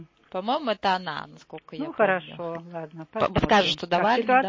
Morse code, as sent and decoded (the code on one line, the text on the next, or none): none